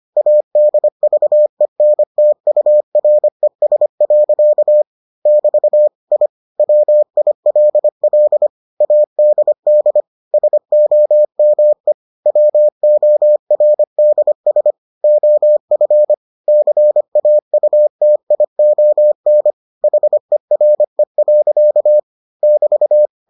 Morse code, heard as a keyed tone, about 25 wpm, 600 Hz. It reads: ADVENTURES. <BT> I WILL ADD SOME WORDS OF CAUTION HERE. <BT>